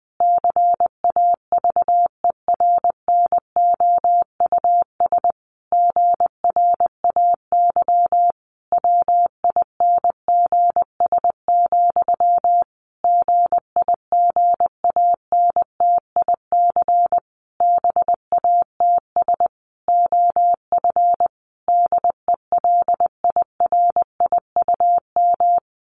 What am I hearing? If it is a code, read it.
Morse code, heard as a keyed tone, about 20 wpm, 700 Hz. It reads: CAVERNOUS GRAY WINGS, GIGANTIC BATS OF DELIRIUM
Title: A Sound Of Thunder 20wpm